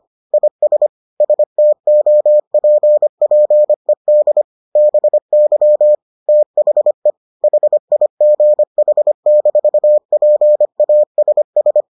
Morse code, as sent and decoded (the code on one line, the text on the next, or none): .. ... / ... - --- .--. .--. -.. / -... -.-- / - .... / .... .. --. .... -....- .--. .- ... ...
IS STOPPED BY THE HIGH-PASS